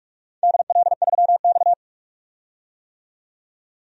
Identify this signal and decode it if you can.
Morse code, heard as a keyed tone, about 46 wpm, 700 Hz. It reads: DL3X
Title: DXCC-100 Call Signs - 4 Characters - 45wpm